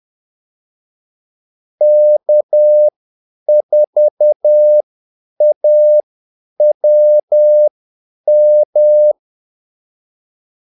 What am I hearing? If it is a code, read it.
Morse code, heard as a keyed tone, about 10 wpm, spaced as for 8 wpm, 600 Hz. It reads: K4AWM
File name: Callsigns-10-8wpm